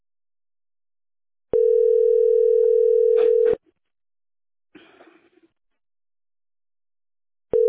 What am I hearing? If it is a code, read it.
Ringback tone (North America), 440 + 480 Hz.
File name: 1beb-467e-b01a